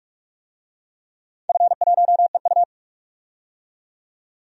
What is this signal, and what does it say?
Morse code, heard as a keyed tone, about 46 wpm, 700 Hz. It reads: F1EU
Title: DXCC-100 Call Signs - 4 Characters - 45wpm